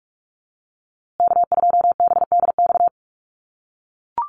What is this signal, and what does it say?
Morse code, heard as a keyed tone, about 44 wpm, 690 Hz. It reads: K2BDX
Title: USA Call Signs Repeated 2x 45wpm